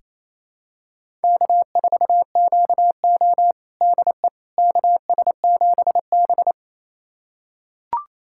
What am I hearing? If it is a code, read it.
Morse code, heard as a keyed tone, about 28 wpm, 700 Hz. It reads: K4QO DE KS7B
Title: USA Callsign DE Callsign - 4 Characters - 28wpm